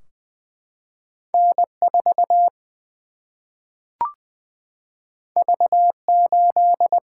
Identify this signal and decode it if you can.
Morse code, heard as a keyed tone, about 20 wpm, 700 Hz. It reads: N 4 V 8
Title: Single-Letter Single-Number 20wpm